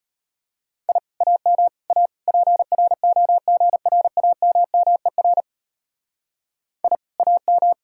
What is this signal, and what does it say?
Morse code, heard as a keyed tone, about 33 wpm, 700 Hz. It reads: I AM A PROGRAMMER I AM